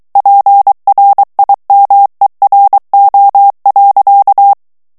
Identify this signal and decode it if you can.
Morse code, heard as a keyed tone, about 23 wpm, 800 Hz. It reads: PRIMERO.
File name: morse mp3 / ESPAGNOL TEXTE EN CLAIR 2 VITESSE 22 MOTS MINUTES